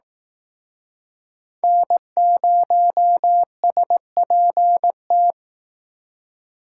Morse code, heard as a keyed tone, about 18 wpm, 700 Hz. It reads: N0SPT
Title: Speed Racing - USA 1x3 Call Signs 15wpm